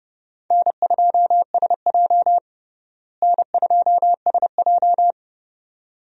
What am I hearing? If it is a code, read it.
Morse code, heard as a keyed tone, about 30 wpm, 700 Hz. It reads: N2SJ N2SJ
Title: USA Call Signs - 4 Characters - Repeated 2x - 30wpm